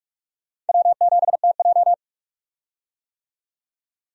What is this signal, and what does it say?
Morse code, heard as a keyed tone, about 47 wpm, 700 Hz. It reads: W7TJ
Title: USA Call Signs - 4 Characters - 45wpm